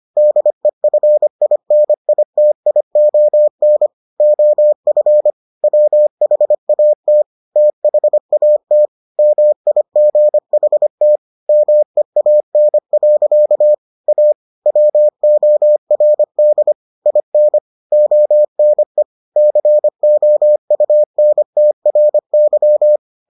Morse code, heard as a keyed tone, about 25 wpm, 600 Hz. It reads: DEFINITION OF WHAT THAT MIGHT MEAN. A WORD IN ONE COUNTRY